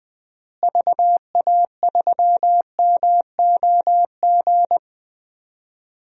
Morse code, heard as a keyed tone, about 21 wpm, 700 Hz. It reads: VA3MOG
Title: DXCC-100 Call Signs - 6 Characters - Repeated 2x - 20wpm